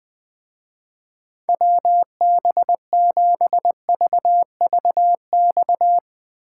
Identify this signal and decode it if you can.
Morse code, heard as a keyed tone, about 22 wpm, 700 Hz. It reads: WB7VVX
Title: Speed Racing - USA 2x3 Call Signs 20wpm